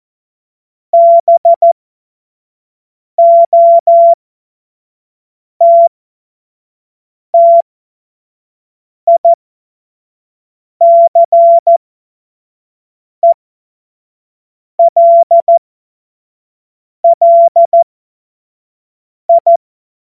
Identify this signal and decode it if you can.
Morse code, heard as a keyed tone, about 14 wpm, spaced as for 5 wpm, 680 Hz. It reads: BOTTICELLI
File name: Morsekurs / Witze / Witz 23 Januar 2024 25 F70